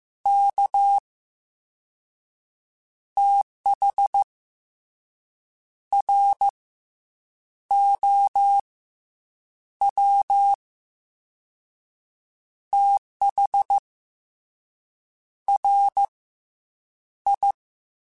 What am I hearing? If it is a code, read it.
Morse code, heard as a keyed tone, about 15 wpm, 800 Hz. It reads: K TH R O W TH R I
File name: Misc / Misc-mp3s / Jumps @13